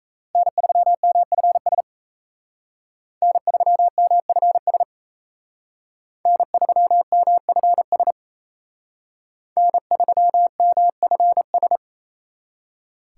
Morse code, heard as a keyed tone, about 35 wpm, 700 Hz. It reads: N3MFS N3MFS N3MFS N3MFS
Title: Speed Racing - USA Call Signs 28wpm